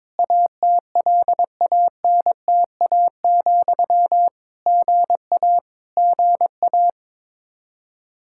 Morse code, heard as a keyed tone, about 22 wpm, 700 Hz. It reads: ATLANTA, GA GA
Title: K1USN SST QSO Practice 22wpm